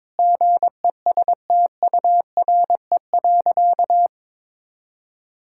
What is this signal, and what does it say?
Morse code, heard as a keyed tone, about 22 wpm, 700 Hz. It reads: GESTURE.
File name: Difficult Sentences with Hints - 22wpm